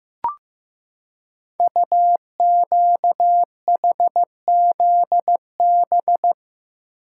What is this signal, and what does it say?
Morse code, heard as a keyed tone, about 15 wpm, 700 Hz. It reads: UQHZB
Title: Code Groups - 5 Letters - 15wpm